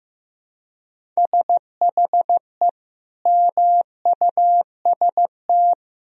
Morse code, heard as a keyed tone, about 15 wpm, 700 Hz. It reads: SHE MUST
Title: Sentences from Top 1000 Words - Review Part 1 - 15wpm